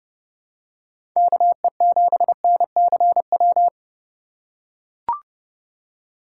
Morse code, heard as a keyed tone, about 32 wpm, 700 Hz. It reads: KE7NCW
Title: USA Call Signs - 6 Characters - 2x Character Spacing - 30wpm